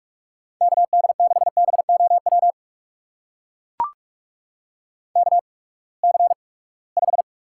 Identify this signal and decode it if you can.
Morse code, heard as a keyed tone, about 45 wpm, spaced as for 25 wpm, 700 Hz. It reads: KD6BOW KC5
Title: USA Call Signs - 6 Characters - 8x Character Spacing - 45wpm